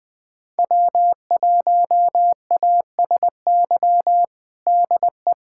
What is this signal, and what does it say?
Morse code, heard as a keyed tone, about 20 wpm, 700 Hz. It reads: W1ASY DE
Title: USA Callsign DE Callsign - 5 Characters - 20wpm